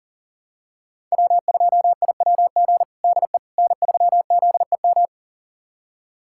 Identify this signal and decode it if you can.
Morse code, heard as a keyed tone, about 37 wpm, 700 Hz. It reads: W2IWG DE N3ZEM
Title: USA Callsign DE Callsign - 5 Characters - 40wpm